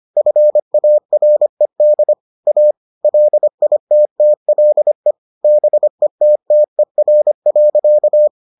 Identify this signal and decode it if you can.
Morse code, heard as a keyed tone, about 25 wpm, 600 Hz. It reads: FARED A LITTLE BETTER.